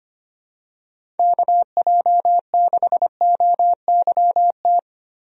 Morse code, heard as a keyed tone, about 25 wpm, 700 Hz. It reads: KJ6OYT
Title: USA Call Signs - 6 Characters - Repeated 2x - 25wpm